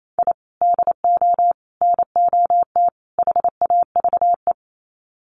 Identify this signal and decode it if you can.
Morse code, heard as a keyed tone, about 28 wpm, 700 Hz. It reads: I DO NOT HAVE
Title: Sets of 4 Words 28wpm